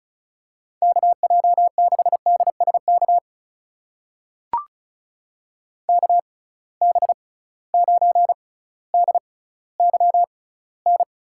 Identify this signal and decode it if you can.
Morse code, heard as a keyed tone, about 36 wpm, spaced as for 18 wpm, 700 Hz. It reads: KJ6DSK KB9DYN
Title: USA Call Signs - 6 Characters - 6x Character Spacing - 35wpm